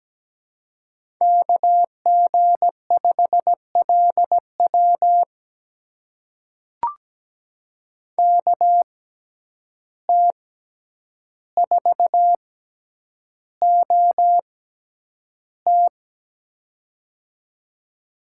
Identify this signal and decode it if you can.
Morse code, heard as a keyed tone, about 17 wpm, spaced as for 9 wpm, 700 Hz. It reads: KG5LW KT4OT
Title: USA Call Signs - 5 Characters - 6x Character Spacing - 17wpm